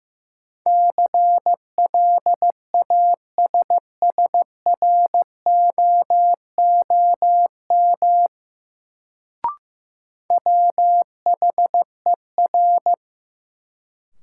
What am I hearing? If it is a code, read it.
Morse code, heard as a keyed tone, about 15 wpm, 700 Hz. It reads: CLASSROOM WHER
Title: Sentences from Top 1500 Words - New Word Drill - 15wpm